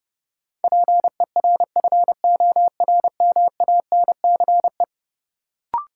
Morse code, heard as a keyed tone, about 30 wpm, 700 Hz. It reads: PERFORMANCE
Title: Sentences from Top 1000 Words - New Word Drill - 30wpm